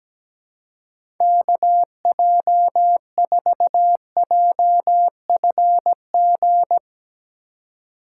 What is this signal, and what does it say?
Morse code, heard as a keyed tone, about 17 wpm, 700 Hz. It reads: KJ4JFG
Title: USA Call Signs - 6 Characters - 2x Character Spacing - 17wpm